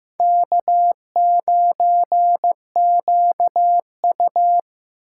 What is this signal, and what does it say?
Morse code, heard as a keyed tone, about 18 wpm, 700 Hz. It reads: K9QU
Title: Speed Racing - USA 1x2 Call Signs 15wpm